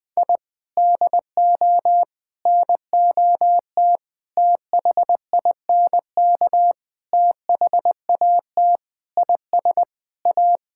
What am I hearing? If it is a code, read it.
Morse code, heard as a keyed tone, about 21 wpm, 700 Hz. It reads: I DO NOT THINK THAT IS A
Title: Sentences with 5-Word Phrases - 20wpm